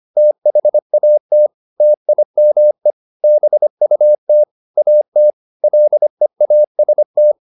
Morse code, heard as a keyed tone, about 25 wpm, 600 Hz. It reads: THAT TIME BUT AT LEAST